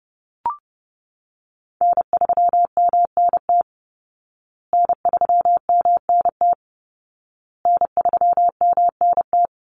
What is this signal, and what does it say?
Morse code, heard as a keyed tone, about 30 wpm, 700 Hz. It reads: N3MNT N3MNT N3MNT
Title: Call Signs Repeated 2x - 30wpm